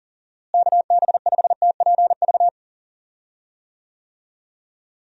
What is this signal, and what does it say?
Morse code, heard as a keyed tone, about 40 wpm, 700 Hz. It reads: KB5TPV
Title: USA Call Signs - 6 Characters - Repeated 3x - 40wpm